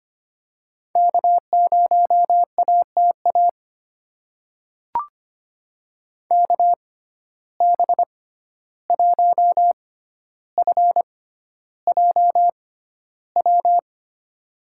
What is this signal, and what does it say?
Morse code, heard as a keyed tone, about 25 wpm, spaced as for 12 wpm, 700 Hz. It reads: K0ATA KB1FJW
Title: USA Call Signs - 6x Character Spacing - 25wpm